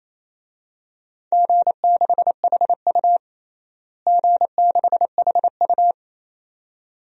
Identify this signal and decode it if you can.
Morse code, heard as a keyed tone, about 28 wpm, 700 Hz. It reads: G6HU G6HU
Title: DXCC-100 Call Signs - 4 Characters - Repeated 2x - 28wpm